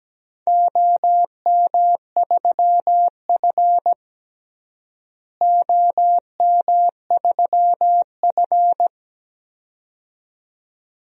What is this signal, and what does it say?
Morse code, heard as a keyed tone, about 17 wpm, 700 Hz. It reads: OM3F OM3F
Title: DXCC-100 Call Signs - 4 Characters - Repeated 2x - 17wpm